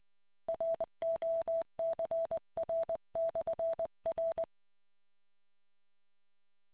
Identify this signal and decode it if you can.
Morse code, heard as a keyed tone, about 20 wpm, 660 Hz. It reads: ROCR/R